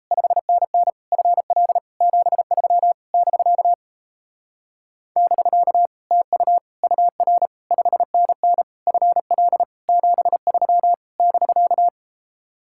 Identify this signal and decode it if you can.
Morse code, heard as a keyed tone, about 38 wpm, 700 Hz. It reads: 5NN FL 73 <BK> <BK> TU UR 5NN FL 73 <BK>